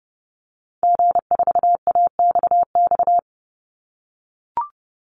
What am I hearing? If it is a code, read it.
Morse code, heard as a keyed tone, about 30 wpm, 700 Hz. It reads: G4AXX
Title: Call Signs - 30wpm